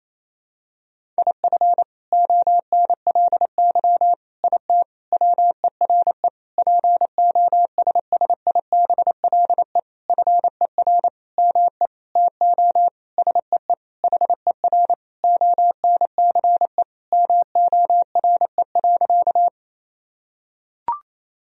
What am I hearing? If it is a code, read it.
Morse code, heard as a keyed tone, about 28 wpm, 700 Hz. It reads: IF ONLY IT WERE POSSIBLE FER ME TO SEE HER ONCE MORE.
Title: Sentences from Top 500 Words and Abbreviations 28wpm